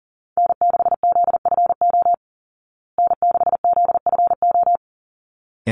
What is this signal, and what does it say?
Morse code, heard as a keyed tone, about 40 wpm, 700 Hz. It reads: N6ZFO N6ZFO
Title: Call Signs Repeated 3x - 40wpm